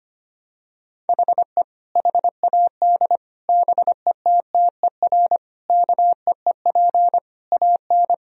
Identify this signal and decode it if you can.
Morse code, heard as a keyed tone, about 25 wpm, 700 Hz. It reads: HE HAD BETTER KEEP AN
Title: Sentences from Top 1000 Words - Review Part 1 - 25wpm